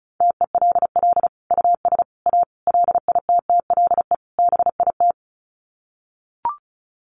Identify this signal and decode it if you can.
Morse code, heard as a keyed tone, about 34 wpm, 700 Hz. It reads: TELL US A LITTLE BIT
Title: Sets of 5 Words Encore2 35wpm